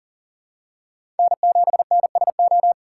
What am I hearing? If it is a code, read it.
Morse code, heard as a keyed tone, about 40 wpm, 700 Hz. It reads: N7NSO
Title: USA Call Signs - 5 Characters - 2x Character Spacing - 40wpm